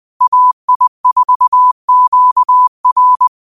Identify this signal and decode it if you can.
Morse code, heard as a keyed tone, about 20 wpm, 1000 Hz. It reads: AI4QR